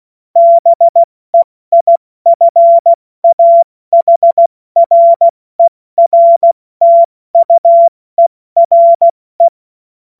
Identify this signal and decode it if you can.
Morse code, heard as a keyed tone, about 16 wpm, 680 Hz. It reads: BEIFAHRERTUERE
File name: Morsekurs / Witze / Witz 31. Mai 2022 70 F80